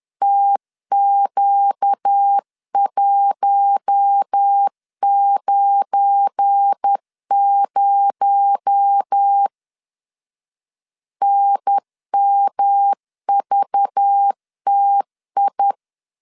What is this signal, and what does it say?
Morse code, heard as a keyed tone, about 10 wpm, 800 Hz. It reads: TQ190 NMVTI
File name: LEZIONI / 7 GF9Z / 7 GF9 10wpm 2747